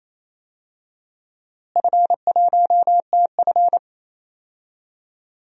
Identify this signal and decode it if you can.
Morse code, heard as a keyed tone, about 28 wpm, 700 Hz. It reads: F1TF
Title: DXCC-100 Call Signs - 4 Characters - Repeated 2x - 28wpm